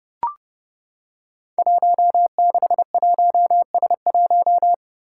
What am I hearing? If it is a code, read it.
Morse code, heard as a keyed tone, about 30 wpm, 700 Hz. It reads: 161S1
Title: Five Challenging Letter-Number Combinations 30wpm